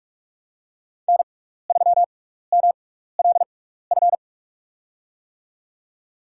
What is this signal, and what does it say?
Morse code, heard as a keyed tone, about 44 wpm, spaced as for 15 wpm, 700 Hz. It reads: N3MLF
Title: USA Call Signs - 6x Character Spacing - 45wpm